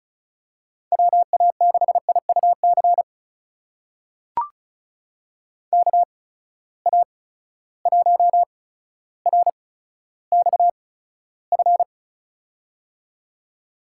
Morse code, heard as a keyed tone, about 36 wpm, spaced as for 15 wpm, 700 Hz. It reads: WA6IUC KA1RXF
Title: USA Call Signs - 6 Characters - 8x Character Spacing - 35wpm